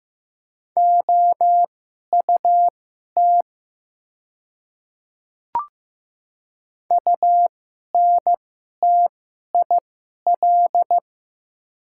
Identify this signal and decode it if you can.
Morse code, heard as a keyed tone, about 15 wpm, spaced as for 11 wpm, 700 Hz. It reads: OUT UNTIL
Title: 2x Character Spacing - Top 300 Words - 15wpm